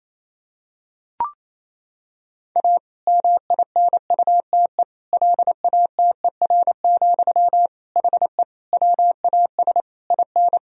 Morse code, heard as a keyed tone, about 28 wpm, 700 Hz. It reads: A MINUTE LATER, HE WAS IN
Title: Sentences from Top 1000 Words - Review Part 4 - 28wpm